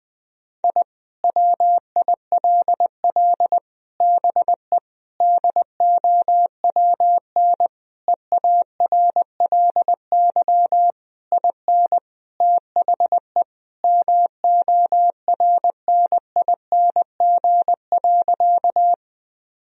Morse code, heard as a keyed tone, about 21 wpm, 700 Hz. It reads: I WILL BE DOWN EARLY IN THE MORNING.
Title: Sentences from Top 1000 Words - Review Part 2 - 20wpm